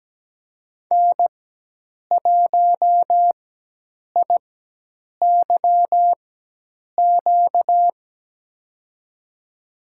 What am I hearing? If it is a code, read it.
Morse code, heard as a keyed tone, about 17 wpm, spaced as for 8 wpm, 700 Hz. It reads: N1IYQ
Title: USA Call Signs - 4x Character Spacing - 17wpm